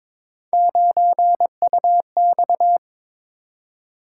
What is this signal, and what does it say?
Morse code, heard as a keyed tone, about 22 wpm, 700 Hz. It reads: 9UX
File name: practice / 22wpm / Three Letter-Number Combinations - 22wpm